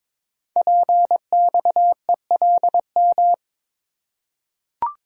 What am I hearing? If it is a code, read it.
Morse code, heard as a keyed tone, about 22 wpm, 700 Hz. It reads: PXELM
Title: Code Groups - 5 Letters - 22wpm